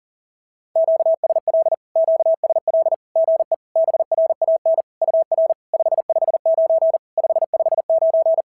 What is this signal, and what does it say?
Morse code, heard as a keyed tone, about 40 wpm, 650 Hz. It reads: QSL QSL GE BRAN UR 559 559
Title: SOTA QSO Practice 40wpm